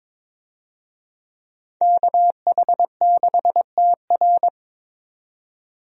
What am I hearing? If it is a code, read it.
Morse code, heard as a keyed tone, about 22 wpm, 700 Hz. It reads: KH6TR
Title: DXCC-100 Call Signs - 5 Characters - 22wpm